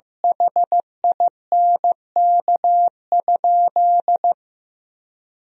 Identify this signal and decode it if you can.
Morse code, heard as a keyed tone, about 16 wpm, 700 Hz. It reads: HINK?
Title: Sentences from Top 200 Words and Abbreviations 15wpm